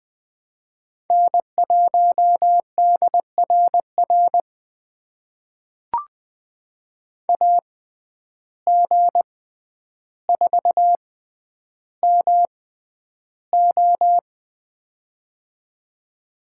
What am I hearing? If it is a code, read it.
Morse code, heard as a keyed tone, about 20 wpm, spaced as for 10 wpm, 700 Hz. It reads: N1DRR AG4MO
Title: USA Call Signs - 5 Characters - 6x Character Spacing - 20wpm